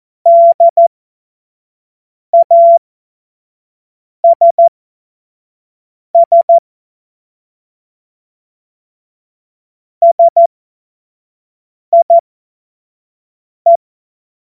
Morse code, heard as a keyed tone, about 14 wpm, spaced as for 5 wpm, 680 Hz. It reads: DASS SIE
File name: Morsekurs / Witze / Witz 04. Januar 2022, 25 F70